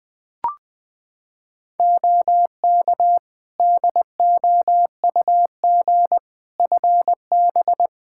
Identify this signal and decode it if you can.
Morse code, heard as a keyed tone, about 21 wpm, 700 Hz. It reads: OK DOUG FB
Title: K1USN SST QSO Practice 20wpm